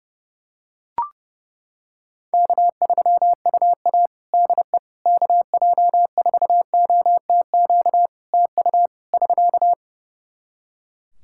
Morse code, heard as a keyed tone, about 30 wpm, 700 Hz. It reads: K3UA DE KJ4OTQ TU <SK>
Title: Practice QSO - Exchange 4 - Ending - 30wpm